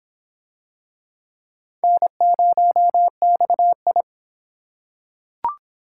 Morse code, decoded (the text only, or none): N0XI